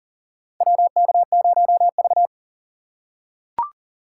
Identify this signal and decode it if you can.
Morse code, heard as a keyed tone, about 40 wpm, 700 Hz. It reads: WK0V
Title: USA Call Signs - 4 Characters - 3x Character Spacing - 40wpm